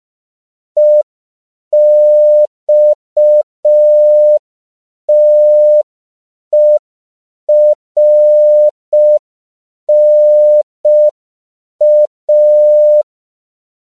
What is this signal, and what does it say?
Morse code, heard as a keyed tone, about 5 wpm, 600 Hz. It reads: EXTERNA
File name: CW / 5PPM